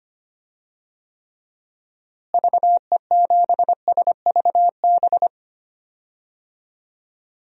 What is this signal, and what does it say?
Morse code, heard as a keyed tone, about 25 wpm, 700 Hz. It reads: VE7SVB
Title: DXCC-100 Call Signs - 6 Characters - Repeated 2x - 25wpm